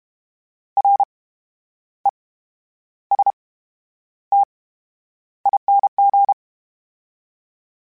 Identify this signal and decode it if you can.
Morse code, heard as a keyed tone, about 32 wpm, spaced as for 11 wpm, 800 Hz. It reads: RESTING
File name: Special INGs / Special INGs-mp3s / VerySlowToFaster Ing @30